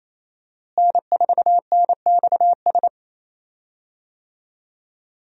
Morse code, heard as a keyed tone, about 27 wpm, 700 Hz. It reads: N4NXS
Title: USA Call Signs - 5 Characters - Repeated 2x - 28wpm